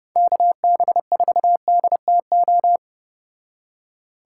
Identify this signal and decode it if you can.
Morse code, heard as a keyed tone, about 34 wpm, 700 Hz. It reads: KB4DTO